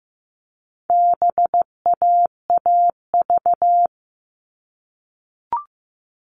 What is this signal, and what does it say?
Morse code, heard as a keyed tone, about 15 wpm, 700 Hz. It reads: BAAV
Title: All Cut Numbers - 4-Digit 15wpm